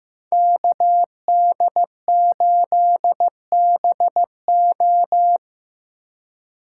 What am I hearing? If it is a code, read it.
Morse code, heard as a keyed tone, about 15 wpm, 700 Hz. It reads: KD8BO
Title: USA Call Signs - 5 Characters - 4x Character Spacing - 15wpm